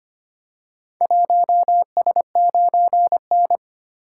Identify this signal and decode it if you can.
Morse code, heard as a keyed tone, about 25 wpm, 700 Hz. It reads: 1S9N
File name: practice / 25wpm / Four Letter-Number Combinations - 25wpm